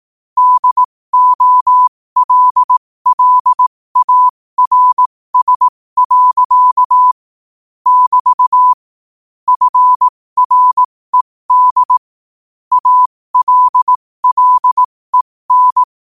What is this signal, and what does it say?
Morse code, heard as a keyed tone, about 18 wpm, spaced as for 15 wpm, 1000 Hz. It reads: DOLLARS. <BT> FRED ALLEN
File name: qotdfiles / 15wpm